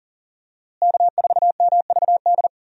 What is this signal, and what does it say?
Morse code, heard as a keyed tone, about 39 wpm, 700 Hz. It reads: K4MVD
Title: USA Call Signs - 5 Characters - Repeated 2x - 40wpm